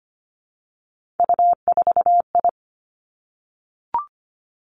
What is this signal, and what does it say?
Morse code, heard as a keyed tone, about 25 wpm, 700 Hz. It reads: U4I
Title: Three Challenging Letter-Number Combinations 25wpm